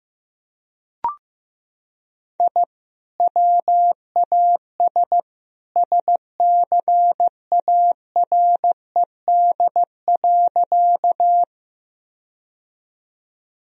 Morse code, heard as a keyed tone, about 16 wpm, 700 Hz. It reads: I WAS SCARED.
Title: Sentences from Top 1000 Words - Review Part 4 - 15wpm